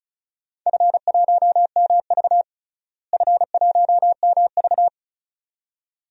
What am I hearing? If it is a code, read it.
Morse code, heard as a keyed tone, about 36 wpm, 700 Hz. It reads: F1MV F1MV
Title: DXCC-100 Call Signs - 4 Characters - Repeated 2x - 35wpm